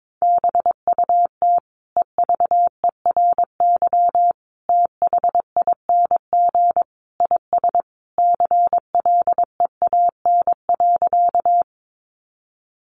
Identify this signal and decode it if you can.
Morse code, heard as a keyed tone, about 22 wpm, 700 Hz. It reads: BUT EVERYTHING IS CLEAN.
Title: Sentences from Top 1100 Words 22wpm